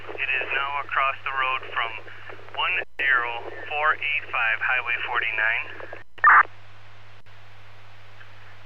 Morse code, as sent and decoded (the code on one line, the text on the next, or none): none